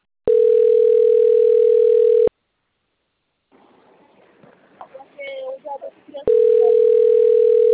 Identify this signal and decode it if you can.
Ringback tone (North America), 440 + 480 Hz, with speech.
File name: rg-1600-unknown- - - .26973